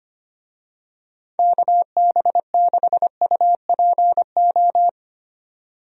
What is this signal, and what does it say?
Morse code, heard as a keyed tone, about 25 wpm, 700 Hz. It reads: KB6UPO
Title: USA Call Signs - 6 Characters - 25wpm